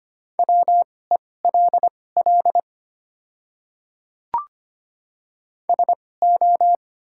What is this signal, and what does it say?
Morse code, heard as a keyed tone, about 25 wpm, spaced as for 18 wpm, 700 Hz. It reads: WELL SO
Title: 2x Character Spacing - Top 200 Words - 25wpm